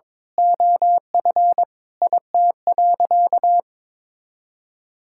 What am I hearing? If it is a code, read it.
Morse code, heard as a keyed tone, about 21 wpm, 700 Hz. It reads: OF IT.
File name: practice / 22wpm / Sentences with 2-Word Phrases - 22wpm